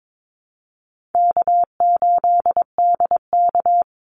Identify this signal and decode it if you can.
Morse code, heard as a keyed tone, about 22 wpm, 700 Hz. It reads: K8DK
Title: Easy Call Signs - 22wpm